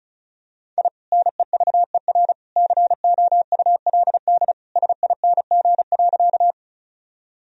Morse code, heard as a keyed tone, about 36 wpm, 700 Hz. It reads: I NEVER COULD SING.
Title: Sentences from Top 1000 Words - Review Part 2 - 35wpm